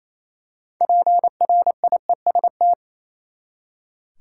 Morse code, heard as a keyed tone, about 27 wpm, 700 Hz. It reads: PRIEST
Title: Six Letter Words - 28wpm